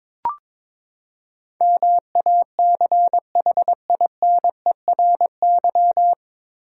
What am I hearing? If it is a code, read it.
Morse code, heard as a keyed tone, about 22 wpm, 700 Hz. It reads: MACHINERY
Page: Morse Code Ninja practice 22wpm